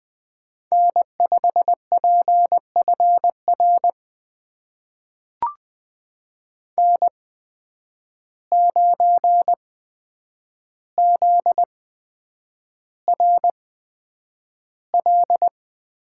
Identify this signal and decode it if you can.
Morse code, heard as a keyed tone, about 20 wpm, spaced as for 9 wpm, 700 Hz. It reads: N5PFR N9ZRL
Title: USA Call Signs - 5 Characters - 8x Character Spacing - 20wpm